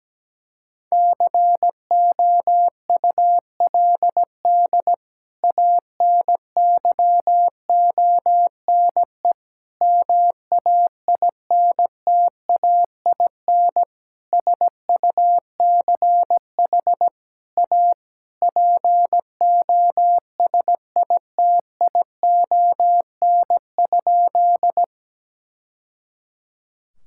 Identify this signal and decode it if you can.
Morse code, heard as a keyed tone, about 17 wpm, 700 Hz. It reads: COULD ANYONE MAINTAIN SUCH A POSITION?
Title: Sentences from Top 1000 Words - Review Part 5 - 17wpm